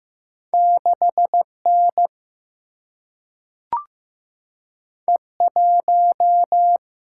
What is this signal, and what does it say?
Morse code, heard as a keyed tone, about 15 wpm, 700 Hz. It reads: 6N E1
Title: Single-Letter Single-Number 15wpm